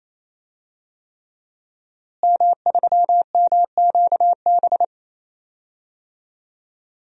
Morse code, heard as a keyed tone, about 27 wpm, 700 Hz. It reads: M3MQB
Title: DXCC-100 Call Signs - 5 Characters - Repeated 3x - 28wpm